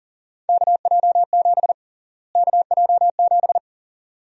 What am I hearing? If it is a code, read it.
Morse code, heard as a keyed tone, about 40 wpm, 700 Hz. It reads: KJ7 KJ7
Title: USA Call Sign Prefixes - Repeated 2x - 40wpm